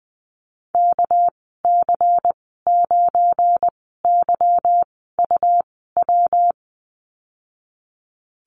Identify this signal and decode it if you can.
Morse code, heard as a keyed tone, about 20 wpm, spaced as for 14 wpm, 700 Hz. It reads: KC9YUW
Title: 245 Lesson 39g - Letter z - Call Signs 20wpm